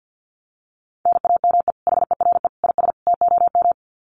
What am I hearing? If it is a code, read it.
Morse code, heard as a keyed tone, about 50 wpm, 700 Hz. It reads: NAME HERE IS TOM
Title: Common QSO Phrases 50wpm